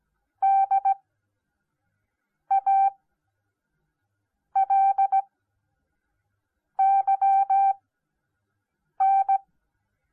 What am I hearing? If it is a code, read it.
Morse code, heard as a keyed tone, about 17 wpm, spaced as for 5 wpm, 780 Hz. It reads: DALYN